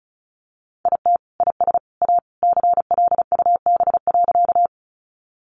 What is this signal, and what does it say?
Morse code, heard as a keyed tone, about 34 wpm, 700 Hz. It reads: IT IS A CLUB.
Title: Sentences from Top 1200 Words 35wpm